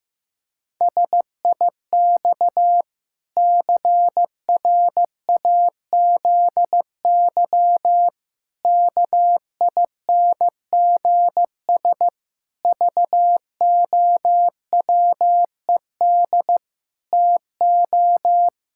Morse code, heard as a keyed tone, about 15 wpm, 700 Hz. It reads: SIX CRAZY KINGS VOWED TO
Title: Pangram Challenge v2 - 15wpm